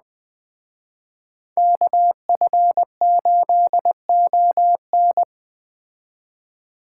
Morse code, heard as a keyed tone, about 21 wpm, 700 Hz. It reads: KF8ON